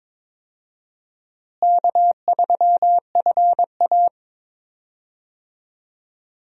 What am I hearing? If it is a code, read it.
Morse code, heard as a keyed tone, about 22 wpm, 700 Hz. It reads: K3FA
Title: USA Call Signs - 4 Characters - Repeated 3x - 22wpm